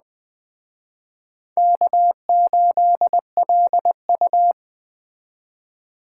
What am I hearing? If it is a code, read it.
Morse code, heard as a keyed tone, about 21 wpm, 700 Hz. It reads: K8LU
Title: Speed Racing - USA 1x2 Call Signs 17wpm